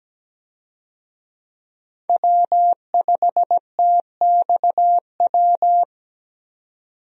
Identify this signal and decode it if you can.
Morse code, heard as a keyed tone, about 17 wpm, 700 Hz. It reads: W5TXW
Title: USA Call Signs - 5 Characters - 17wpm